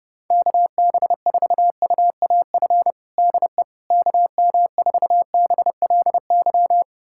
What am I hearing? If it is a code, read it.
Morse code, heard as a keyed tone, about 32 wpm, 700 Hz. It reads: KB4UAF DE KM4BLY
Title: USA Callsign DE Callsign 30wpm